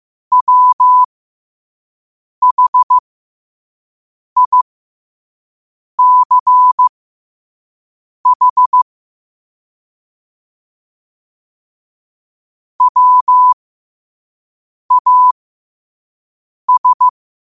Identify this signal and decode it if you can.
Morse code, heard as a keyed tone, about 15 wpm, spaced as for 5 wpm, 1000 Hz. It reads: WHICH WAS
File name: qotdfiles / 5wpm